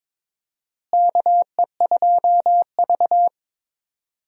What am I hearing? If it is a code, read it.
Morse code, heard as a keyed tone, about 22 wpm, 700 Hz. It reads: KE2V
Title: USA Call Signs - 4 Characters - Repeated 2x - 22wpm